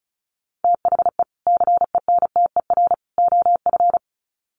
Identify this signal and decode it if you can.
Morse code, heard as a keyed tone, about 34 wpm, 700 Hz. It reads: THE CENTER OF
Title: Sets of 3 Words 35wpm